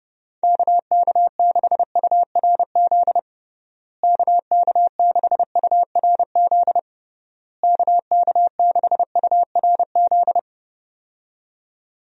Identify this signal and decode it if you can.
Morse code, heard as a keyed tone, about 32 wpm, 700 Hz. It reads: KK6URZ KK6URZ KK6URZ
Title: USA Call Signs - 6 Characters - Repeated 3x - 30wpm